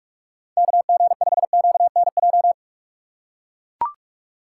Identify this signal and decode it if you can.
Morse code, heard as a keyed tone, about 41 wpm, 700 Hz. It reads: KG5QNJ
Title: USA Call Signs - 6 Characters - 2x Character Spacing - 45wpm